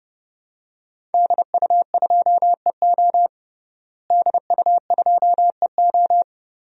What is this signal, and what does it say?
Morse code, heard as a keyed tone, about 30 wpm, 700 Hz. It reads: DU2EO DU2EO
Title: DXCC-100 Call Signs - 5 Characters - Repeated 3x - 30wpm